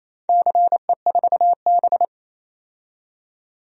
Morse code, heard as a keyed tone, about 28 wpm, 700 Hz. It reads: CE4B
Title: DXCC-100 Call Signs - 4 Characters - 28wpm